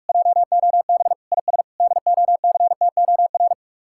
Morse code, heard as a keyed tone, about 49 wpm, 700 Hz. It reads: JOB IS DOCTOR